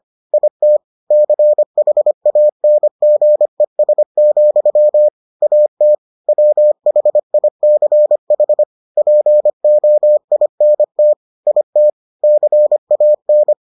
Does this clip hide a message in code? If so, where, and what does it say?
Yes: Morse code from 0.3 to 13.5 s, heard as a keyed tone, about 25 wpm, 600 Hz. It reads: IT CHANGES, AT WHICH POINT IT CAN